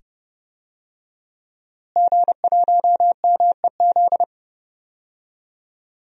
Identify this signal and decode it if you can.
Morse code, heard as a keyed tone, about 31 wpm, 700 Hz. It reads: G1MEZ
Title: DXCC-100 Call Signs - 5 Characters - Repeated 2x - 30wpm